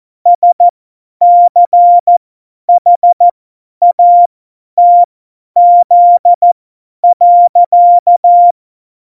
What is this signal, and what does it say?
Morse code, heard as a keyed tone, about 14 wpm, spaced as for 10 wpm, 700 Hz. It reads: SCHATZ.
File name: Morsekurs / Witze / Witz 21. November 2017, 50 F70